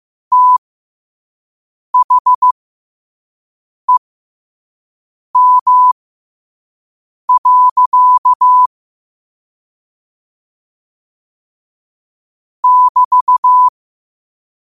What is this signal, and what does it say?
Morse code, heard as a keyed tone, about 15 wpm, spaced as for 5 wpm, 1000 Hz. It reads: THEM. <BT>
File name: qotdfiles / 5wpm